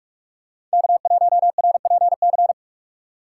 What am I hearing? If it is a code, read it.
Morse code, heard as a keyed tone, about 48 wpm, 700 Hz. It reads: K1RPC
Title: USA Call Signs - 2x Character Spacing - 45wpm